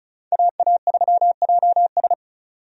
Morse code, heard as a keyed tone, about 34 wpm, 700 Hz. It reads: AA3JS
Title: USA Call Signs - 5 Characters - 35wpm